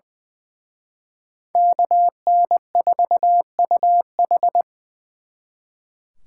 Morse code, heard as a keyed tone, about 24 wpm, 700 Hz. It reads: KN4UH
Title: Speed Racing - USA Call Signs 20wpm